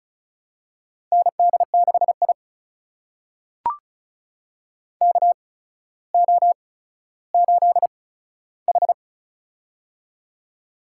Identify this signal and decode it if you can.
Morse code, heard as a keyed tone, about 36 wpm, spaced as for 15 wpm, 700 Hz. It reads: ND6I KO8H
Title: USA Call Signs - 4 Characters - 8x Character Spacing - 35wpm